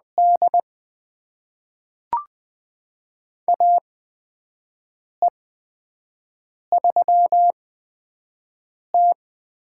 Morse code, heard as a keyed tone, about 20 wpm, spaced as for 5 wpm, 700 Hz. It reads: D AE3T
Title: USA Call Signs - 4 Characters - 8x Character Spacing - 20wpm